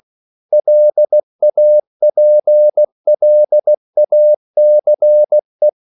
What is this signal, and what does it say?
Morse code, heard as a keyed tone, about 16 wpm, 600 Hz. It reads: LAPLACE
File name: CW / Practice / EnglishQuotations0416